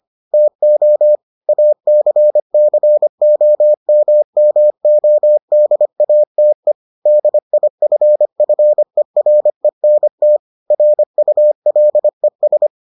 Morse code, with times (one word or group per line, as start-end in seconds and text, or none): 0.3-1.2 TO
1.5-6.7 ACCOMMODATE
7.1-10.4 DIFFERENT
10.7-12.7 RULES